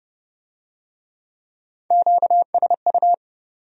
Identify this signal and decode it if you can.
Morse code, heard as a keyed tone, about 29 wpm, 700 Hz. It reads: QSU